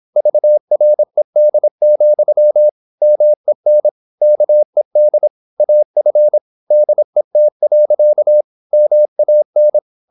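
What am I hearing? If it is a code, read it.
Morse code, heard as a keyed tone, about 26 wpm, 600 Hz. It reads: VRED, MEN KED AF DET. MAN
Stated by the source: High-speed (QRQ) Morse at 26 wpm.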